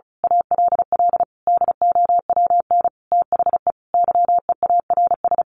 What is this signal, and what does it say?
Morse code, heard as a keyed tone, about 34 wpm, 700 Hz. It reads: ALL DOWN THE YEARS
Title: A Sound Of Thunder 35wpm